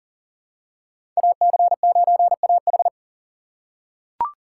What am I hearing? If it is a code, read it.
Morse code, heard as a keyed tone, about 38 wpm, 690 Hz. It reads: AC9AH
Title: USA Call Signs - 4x Character Spacing - 40wpm